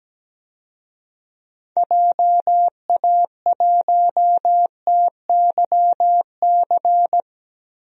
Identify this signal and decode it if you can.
Morse code, heard as a keyed tone, about 17 wpm, 700 Hz. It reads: JA1TYC
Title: DXCC-100 Call Signs - 6 Characters - Repeated 2x - 17wpm